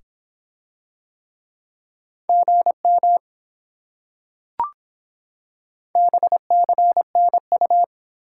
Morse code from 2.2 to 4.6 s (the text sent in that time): GM